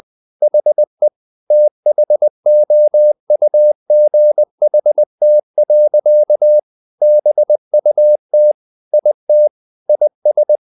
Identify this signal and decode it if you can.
Morse code, heard as a keyed tone, about 20 wpm, 600 Hz. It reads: HE THOUGHT. BUT IT IS
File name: the old man and the sea / the old man and the sea part 1 20wpm